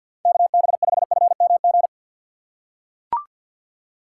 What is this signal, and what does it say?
Morse code, heard as a keyed tone, about 53 wpm, 700 Hz. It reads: KB5FMG
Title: USA Call Signs - 6 Characters - 6x Character Spacing - 50wpm